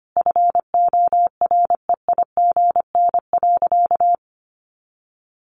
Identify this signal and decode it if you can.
Morse code, heard as a keyed tone, about 25 wpm, 700 Hz. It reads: FOREIGN.
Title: Sentences from Top 1400 Words 25wpm